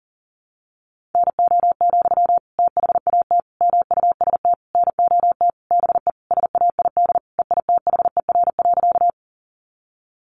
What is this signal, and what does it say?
Morse code, heard as a keyed tone, about 40 wpm, 700 Hz. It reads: NO, THAT MUST NOT BE SAID EITHER.
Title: Sentences from Top 600 Words 40wpm